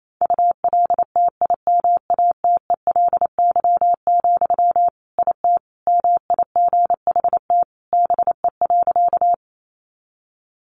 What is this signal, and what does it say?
Morse code, heard as a keyed tone, about 28 wpm, 700 Hz. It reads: ULTIMATELY, IT MIGHT BE.
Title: Sentences from Top 1200 Words 28wpm